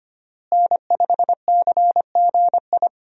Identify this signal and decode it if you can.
Morse code, heard as a keyed tone, about 25 wpm, 700 Hz. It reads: N5CGI
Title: USA Call Signs - 5 Characters - Repeated 2x - 25wpm